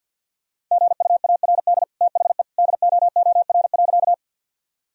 Morse code, heard as a keyed tone, about 50 wpm, 700 Hz. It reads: GUARD THE DOOR.
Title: Sentences from Top 1000 Words - Review Part 4 - 50wpm